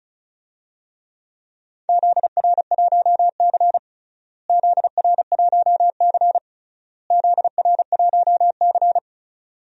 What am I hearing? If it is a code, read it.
Morse code, heard as a keyed tone, about 34 wpm, 700 Hz. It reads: ZR1C ZR1C ZR1C
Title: DXCC-100 Call Signs - 4 Characters - Repeated 3x - 35wpm